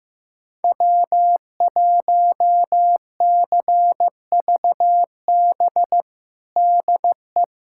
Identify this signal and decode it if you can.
Morse code, heard as a keyed tone, about 15 wpm, 700 Hz. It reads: W1CVB DE
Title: USA Callsign DE Callsign - 5 Characters - 15wpm